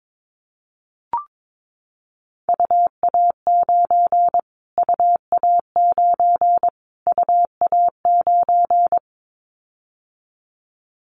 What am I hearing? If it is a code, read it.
Morse code, heard as a keyed tone, about 22 wpm, 700 Hz. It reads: UA9 UA9 UA9
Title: Callsign Prefixes for Top 25 DXCC - Repeated 3x - 22wpm